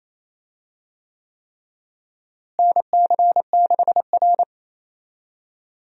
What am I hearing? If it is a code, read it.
Morse code, heard as a keyed tone, about 27 wpm, 700 Hz. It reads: NC6R
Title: USA Call Signs - 4 Characters - Repeated 3x - 28wpm